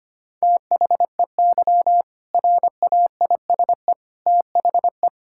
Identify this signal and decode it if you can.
Morse code, heard as a keyed tone, about 25 wpm, 700 Hz. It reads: THEY RAISE THE
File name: practice / 25wpm / All-Age Appropriate Jokes - 25wpm